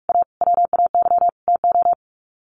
Morse code, heard as a keyed tone, about 44 wpm, 700 Hz. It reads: A WAY TO
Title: Sets of 5 Words 45wpm